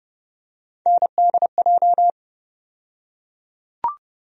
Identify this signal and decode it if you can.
Morse code, heard as a keyed tone, about 32 wpm, 700 Hz. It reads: NDJ
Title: USA Call Sign Postfixes - 3 Letters 30wpm